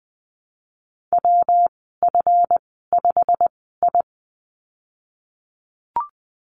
Morse code, heard as a keyed tone, about 20 wpm, spaced as for 15 wpm, 700 Hz. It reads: WF5I
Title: 147 Lesson 24g - Letter y - Call Signs 20wpm